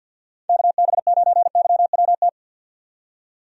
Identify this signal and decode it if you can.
Morse code, heard as a keyed tone, about 51 wpm, 700 Hz. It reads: KB9YWT
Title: USA Call Signs - 6 Characters - 50wpm